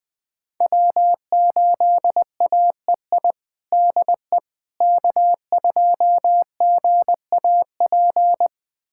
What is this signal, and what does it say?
Morse code, heard as a keyed tone, about 21 wpm, 700 Hz. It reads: W8AEI DE K2GAP
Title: USA Callsign DE Callsign - 5 Characters - 20wpm